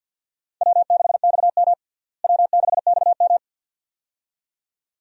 Morse code, heard as a keyed tone, about 48 wpm, 700 Hz. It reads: W6XM W6XM
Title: USA Call Signs - 4 Characters - Repeated 2x - 50wpm